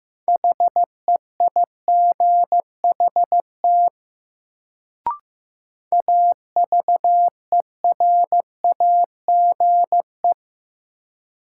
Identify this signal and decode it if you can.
Morse code, heard as a keyed tone, about 15 wpm, 700 Hz. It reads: HEIGHT AVERAGE
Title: Sets of 5 Words - New Word Drill - 15wpm